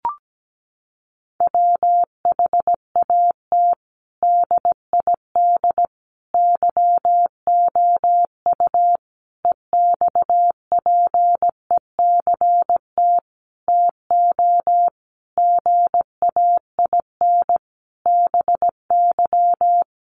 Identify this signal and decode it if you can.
Morse code, heard as a keyed tone, about 17 wpm, 700 Hz. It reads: WHAT DID YOU EXPECT TO GAIN BY
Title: Sentences from Top 1100 Words 17wpm